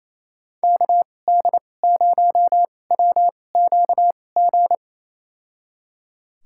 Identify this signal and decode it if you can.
Morse code, heard as a keyed tone, about 28 wpm, spaced as for 20 wpm, 700 Hz. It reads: KD0WQG
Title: USA Call Signs - 6 Characters - 2x Character Spacing - 28wpm